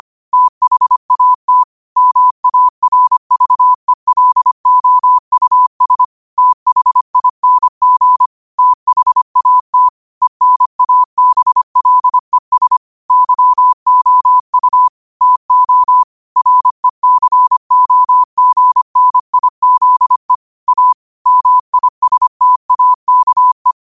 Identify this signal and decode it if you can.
Morse code, heard as a keyed tone, about 25 wpm, 1000 Hz. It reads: THAT MARVELOUS THING THAT ENABLES YOU TO RECOGNIZE A MISTAKE
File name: qotdfiles / 25wpm